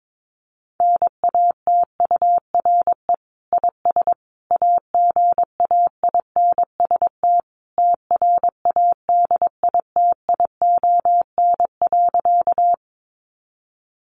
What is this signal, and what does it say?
Morse code, heard as a keyed tone, about 22 wpm, 700 Hz. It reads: NATURE IS AGAINST TRADITION.
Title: Sentences from Top 1500 Words 22wpm